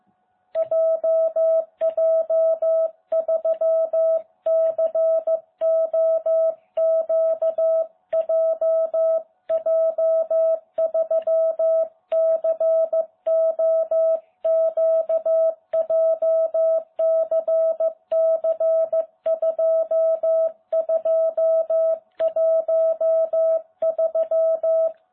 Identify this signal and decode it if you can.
Morse code, heard as a keyed tone, about 15 wpm, 660 Hz. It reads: JJ3COQ JJ3COQ JCC2213